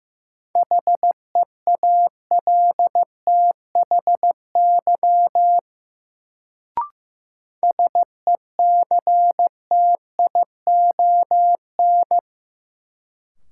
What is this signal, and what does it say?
Morse code, heard as a keyed tone, about 15 wpm, 700 Hz. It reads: HEALTHY SECTION
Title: Sentences from Top 1000 Words - Review Part 5 - New Word Drill - 15wpm